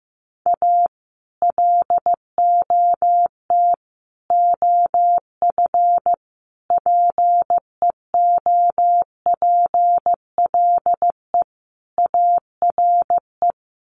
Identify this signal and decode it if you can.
Morse code, heard as a keyed tone, about 15 wpm, 700 Hz. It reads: A LOT OF PEOPLE ARE
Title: Sets of 5 Words 15wpm